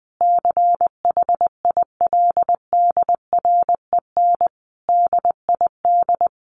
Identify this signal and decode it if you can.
Morse code, heard as a keyed tone, about 20 wpm, 700 Hz. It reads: CHILDREN DID
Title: Sentences from Top 400 Words 20wpm